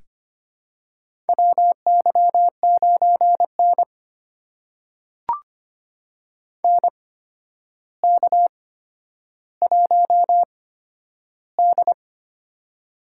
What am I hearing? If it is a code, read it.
Morse code, heard as a keyed tone, about 25 wpm, spaced as for 11 wpm, 700 Hz. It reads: WY9N NK1D
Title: USA Call Signs - 4 Characters - 8x Character Spacing - 25wpm